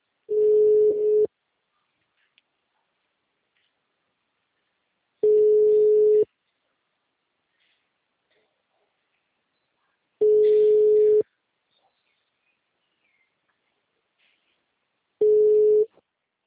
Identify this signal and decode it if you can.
Ringback tone (Europe, 425 Hz).
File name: out- -2014- - - .50